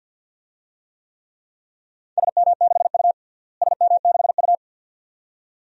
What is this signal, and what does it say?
Morse code, heard as a keyed tone, about 47 wpm, 700 Hz. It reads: SM6U SM6U
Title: DXCC-100 Call Signs - 4 Characters - Repeated 2x - 50wpm